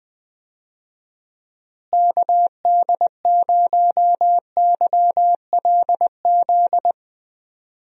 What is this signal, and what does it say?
Morse code, heard as a keyed tone, about 20 wpm, 700 Hz. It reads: KD0YLZ
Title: USA Call Signs - 6 Characters - Repeated 2x - 20wpm